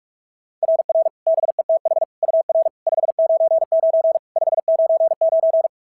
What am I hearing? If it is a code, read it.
Morse code, heard as a keyed tone, about 44 wpm, 690 Hz. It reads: RR BETH UR 599 599
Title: SOTA QSO Practice 45wpm